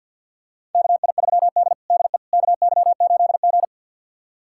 Morse code, heard as a keyed tone, about 48 wpm, 700 Hz. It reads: KI3D DE KY8G
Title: USA Callsign DE Callsign - 4 Characters - 50wpm